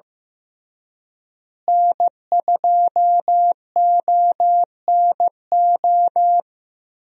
Morse code, heard as a keyed tone, about 16 wpm, 700 Hz. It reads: N2ONO
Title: USA Call Signs - 5 Characters - Repeated 3x - 15wpm